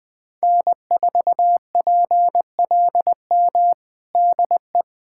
Morse code, heard as a keyed tone, about 20 wpm, 700 Hz. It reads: N4PLM DE
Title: USA Callsign DE Callsign 20wpm